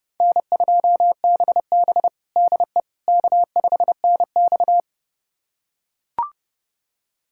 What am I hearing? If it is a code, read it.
Morse code, heard as a keyed tone, about 30 wpm, 700 Hz. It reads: N2BB DE K5NX
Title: USA Callsign DE Callsign - 4 Characters - 30wpm